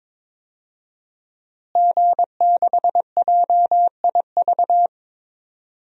Morse code, heard as a keyed tone, about 22 wpm, 700 Hz. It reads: G6JIV
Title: DXCC-100 Call Signs - 5 Characters - Repeated 3x - 22wpm